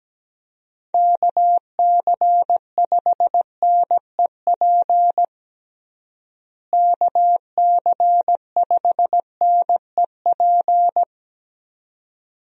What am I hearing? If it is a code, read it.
Morse code, heard as a keyed tone, about 17 wpm, 700 Hz. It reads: KC5NEP KC5NEP
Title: USA Call Signs - 6 Characters - Repeated 2x - 17wpm